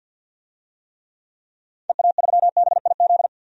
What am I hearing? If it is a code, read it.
Morse code, heard as a keyed tone, about 52 wpm, 700 Hz. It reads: EA3BIZ
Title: DXCC-100 Call Signs - 6 Characters - 50wpm